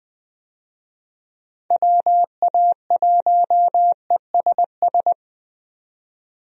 Morse code, heard as a keyed tone, about 20 wpm, 700 Hz. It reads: WA1ESS
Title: USA Call Signs - 6 Characters - 20wpm